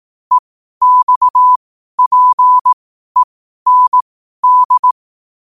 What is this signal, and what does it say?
Morse code, heard as a keyed tone, about 18 wpm, spaced as for 12 wpm, 1000 Hz. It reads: EXPEND
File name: qotdfiles / 12wpm